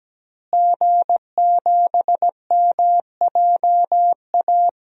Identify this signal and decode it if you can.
Morse code, heard as a keyed tone, about 17 wpm, 700 Hz. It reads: G7MJA
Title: DXCC-100 Call Signs - 5 Characters - Repeated 2x - 17wpm